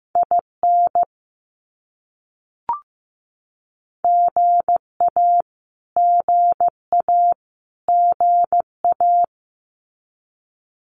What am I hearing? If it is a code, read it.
Morse code, heard as a keyed tone, about 15 wpm, 700 Hz. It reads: IN GA GA GA
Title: US States Repeated 3x 15wpm